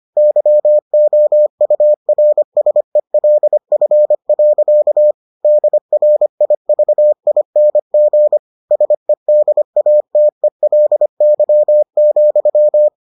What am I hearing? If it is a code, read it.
Morse code, heard as a keyed tone, about 25 wpm, 600 Hz. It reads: YOURSELF. DRIVING SEDATELY,